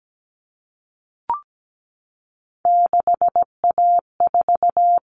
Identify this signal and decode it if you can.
Morse code, heard as a keyed tone, about 17 wpm, 700 Hz. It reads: 6A4
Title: All Cut Numbers - 3-Digit 17wpm